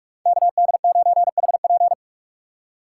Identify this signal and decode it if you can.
Morse code, heard as a keyed tone, about 46 wpm, 700 Hz. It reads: KD9HP
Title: USA Call Signs - 5 Characters - 4x Character Spacing - 45wpm